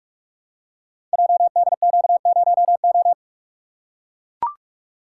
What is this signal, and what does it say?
Morse code, heard as a keyed tone, about 44 wpm, 700 Hz. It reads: JDQ0O
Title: Code Groups - 5 Letter:Numbers - 45wpm